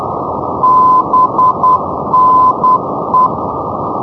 Morse code, heard as a keyed tone, about 10 wpm, 1020 Hz, with noise. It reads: BNE
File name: camerica / lca / lca-25 ndb hewanorra bne-305 np-mrt-2013-k